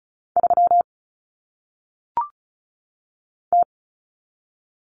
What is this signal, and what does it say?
Morse code, heard as a keyed tone, about 36 wpm, 700 Hz. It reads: 3 T
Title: Common Cut Numbers - 1-Digit 35wpm